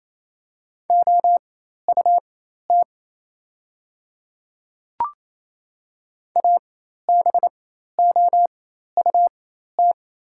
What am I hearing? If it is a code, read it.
Morse code, heard as a keyed tone, about 28 wpm, spaced as for 13 wpm, 690 Hz. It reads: OUT ABOUT
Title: 4x Character Spacing - Top 100 Words - 28wpm